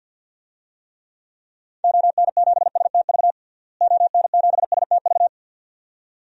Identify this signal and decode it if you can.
Morse code, heard as a keyed tone, about 47 wpm, 700 Hz. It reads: ON7STV ON7STV
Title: DXCC-100 Call Signs - 6 Characters - Repeated 2x - 50wpm